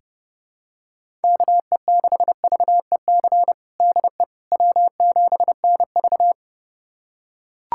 Morse code, heard as a keyed tone, about 30 wpm, 700 Hz. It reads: KE6VEC DE W7NV
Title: USA Callsign DE Callsign 30wpm